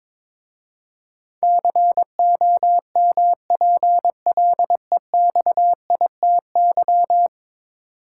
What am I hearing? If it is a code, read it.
Morse code, heard as a keyed tone, about 23 wpm, 700 Hz. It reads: COMPLEXITY
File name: practice / 22wpm / 10 Letter Words - First 3 Letters Given - 22wpm